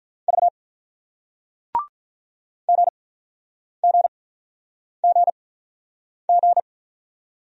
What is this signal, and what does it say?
Morse code, heard as a keyed tone, about 48 wpm, spaced as for 9 wpm, 700 Hz. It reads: V GGGG